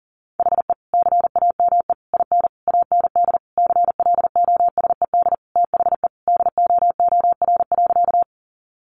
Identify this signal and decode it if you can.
Morse code, heard as a keyed tone, about 40 wpm, 700 Hz. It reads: HE CAME IN AND CLOSED THE DOOR.
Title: Sentences from Top 1400 Words 40wpm